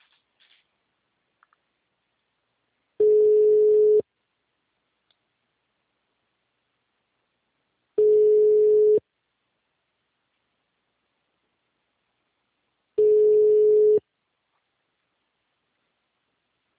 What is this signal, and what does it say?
Telephone call progress tone (Europe, 425 Hz), heard as a telephone network's tone, about 420 Hz, with speech. Ringback tone.